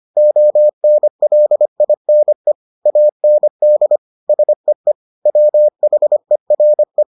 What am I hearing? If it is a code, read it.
Morse code, heard as a keyed tone, about 25 wpm, 600 Hz. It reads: ONLINE AND SEE WHERE